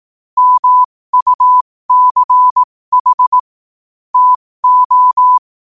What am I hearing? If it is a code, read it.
Morse code, heard as a keyed tone, about 18 wpm, spaced as for 15 wpm, 1000 Hz. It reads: MUCH TO
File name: qotdfiles / 15wpm